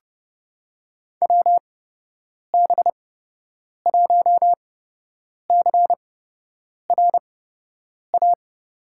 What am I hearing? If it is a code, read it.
Morse code, heard as a keyed tone, about 30 wpm, spaced as for 8 wpm, 700 Hz. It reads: WB1CRA
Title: USA Call Signs - 6 Characters - 8x Character Spacing - 30wpm